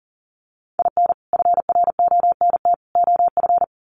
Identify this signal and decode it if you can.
Morse code, heard as a keyed tone, about 40 wpm, 700 Hz. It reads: IN FRONT OF
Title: Sets of 5 Words Encore2 40wpm